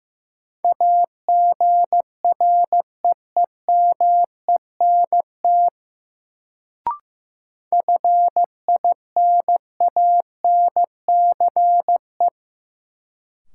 Morse code, heard as a keyed tone, about 15 wpm, 700 Hz. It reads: AGREEMENT FINANCE
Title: Sentences from Top 1500 Words - New Word Drill - 15wpm